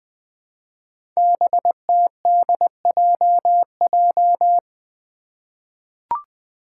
Morse code, heard as a keyed tone, about 20 wpm, 700 Hz. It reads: BTDJJ
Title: Five Challenging Letter-Number Combinations 20wpm